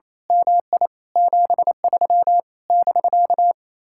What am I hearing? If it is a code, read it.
Morse code, heard as a keyed tone, about 27 wpm, 700 Hz. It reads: MI 73 <BK>
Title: Speed-Racing Common QSO Phrases Encore - 28wpm